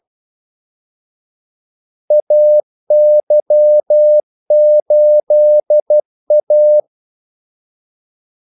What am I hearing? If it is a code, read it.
Morse code, heard as a keyed tone, about 12 wpm, 600 Hz. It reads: AY8A
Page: 2 ekstra ordmellemrum mellem kaldesignalerne